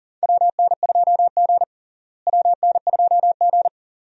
Morse code, heard as a keyed tone, about 40 wpm, 700 Hz. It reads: WN2G WN2G
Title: USA Call Signs - 4 Characters - Repeated 2x - 40wpm